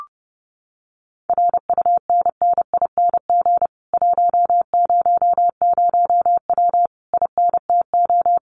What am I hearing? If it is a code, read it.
Morse code, heard as a keyed tone, about 28 wpm, 700 Hz. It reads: RUNNING 100W INTO
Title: Common QSO Phrases Encore 30wpm